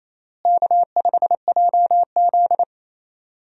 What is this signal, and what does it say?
Morse code, heard as a keyed tone, about 28 wpm, 700 Hz. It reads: K5JZ
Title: USA Call Signs - 4 Characters - Repeated 2x - 28wpm